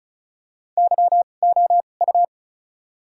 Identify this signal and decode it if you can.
Morse code, heard as a keyed tone, about 36 wpm, spaced as for 25 wpm, 700 Hz. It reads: YOU
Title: 2x Character Spacing - Top 100 Words - 35wpm